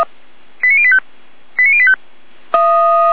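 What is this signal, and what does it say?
Five-tone selective call (ZVEI), heard as quick runs of five tones, tones of about 70 ms. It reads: 89085 89085
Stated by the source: ZVEI five-tone selective call received on an SDR.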